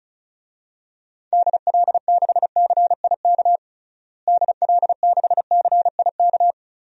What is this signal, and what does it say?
Morse code, heard as a keyed tone, about 34 wpm, 700 Hz. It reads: DL6CIK DL6CIK
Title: DXCC-100 Call Signs - 6 Characters - Repeated 3x - 35wpm